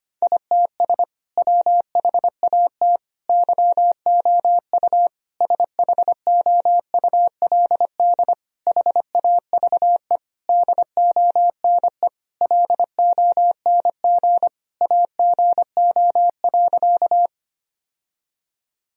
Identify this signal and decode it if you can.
Morse code, heard as a keyed tone, about 25 wpm, 700 Hz. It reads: ITS WHAT YOU SHOULD HAVE DONE LONG AGO.
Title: Sentences from Top 1000 Words - Review Part 3 - 25wpm